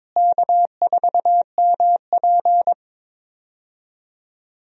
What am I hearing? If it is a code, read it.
Morse code, heard as a keyed tone, about 22 wpm, 700 Hz. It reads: K4MP
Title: USA Call Signs - 4 Characters - Repeated 3x - 22wpm